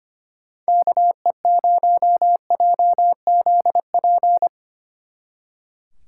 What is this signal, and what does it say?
Morse code, heard as a keyed tone, about 32 wpm, 700 Hz. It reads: KE0JZP